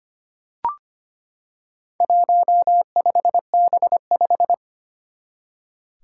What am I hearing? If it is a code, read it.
Morse code, heard as a keyed tone, about 25 wpm, 700 Hz. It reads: 15B5
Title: Four Challenging Letter-Number Combinations 25wpm